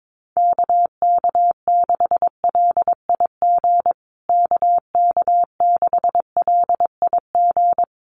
Morse code, heard as a keyed tone, about 22 wpm, 700 Hz. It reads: KK6LIG KK6LIG
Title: USA Call Signs Repeated 3x 22wpm